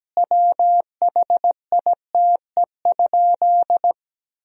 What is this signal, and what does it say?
Morse code, heard as a keyed tone, about 17 wpm, 700 Hz. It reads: WHITE?
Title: Sentences from Top 1000 Words - Review Part 1 - 17wpm